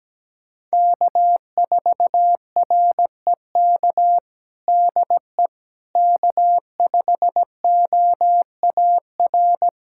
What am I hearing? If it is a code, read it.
Morse code, heard as a keyed tone, about 17 wpm, 700 Hz. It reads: K4REK DE K5OAR
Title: USA Callsign DE Callsign - 5 Characters - 17wpm